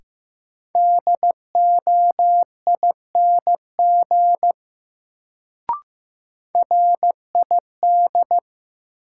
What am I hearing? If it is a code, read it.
Morse code, heard as a keyed tone, about 15 wpm, 700 Hz. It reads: DOING RID
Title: Sets of 4 Words - New Word Drill - 15wpm